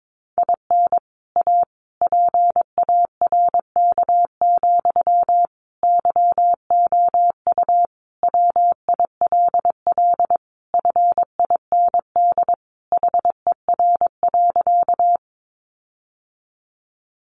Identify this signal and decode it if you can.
Morse code, heard as a keyed tone, about 22 wpm, 700 Hz. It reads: IN A PARK, YOU WILL FIND HER.
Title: Sentences from Top 1300 Words 22wpm